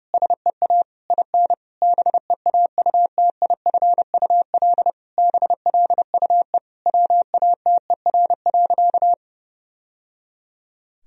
Morse code, heard as a keyed tone, about 32 wpm, 700 Hz. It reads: SEA IN BEAUTIFUL BLUE WATER.
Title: Sentences from Top 1000 Words - Review Part 3 - 30wpm